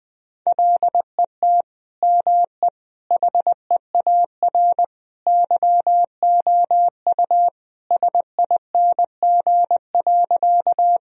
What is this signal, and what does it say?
Morse code, heard as a keyed tone, about 20 wpm, 700 Hz. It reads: LET ME HEAR YOU SING.
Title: Sentences from Top 1000 Words - Review Part 2 - 20wpm